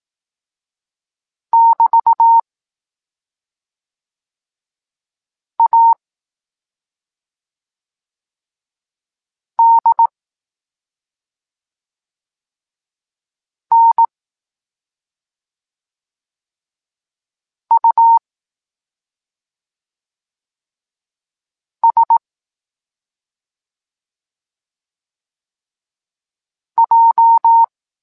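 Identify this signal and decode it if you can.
Morse code, heard as a keyed tone, about 18 wpm, 920 Hz. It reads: <BT> A D N U S J
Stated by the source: Evenly spaced Morse lesson text 21 for learning letters G and Z, from SP9MOA Niepołomicki Klub Krótkofalowców.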